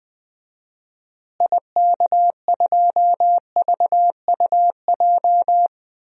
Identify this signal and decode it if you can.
Morse code, heard as a keyed tone, about 20 wpm, 700 Hz. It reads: IK2VUJ
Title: DXCC-100 Call Signs - 6 Characters - Repeated 3x - 20wpm